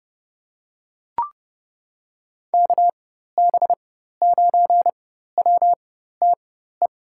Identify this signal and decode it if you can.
Morse code, heard as a keyed tone, about 31 wpm, spaced as for 14 wpm, 700 Hz. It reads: KB9WTE
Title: USA Call Signs - 6 Characters - 4x Character Spacing - 30wpm